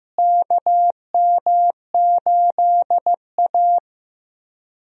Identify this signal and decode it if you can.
Morse code, heard as a keyed tone, about 15 wpm, 700 Hz. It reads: KM8A
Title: USA Call Signs - 4 Characters - Repeated 3x - 15wpm